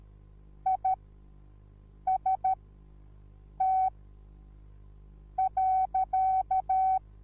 Morse code, heard as a keyed tone, about 13 wpm, spaced as for 6 wpm, 760 Hz. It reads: IST.